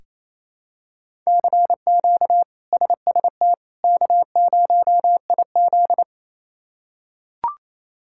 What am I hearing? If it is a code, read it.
Morse code, heard as a keyed tone, about 27 wpm, 700 Hz. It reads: CQ SST K0IZ
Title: K1USN SST QSO Practice - 28wpm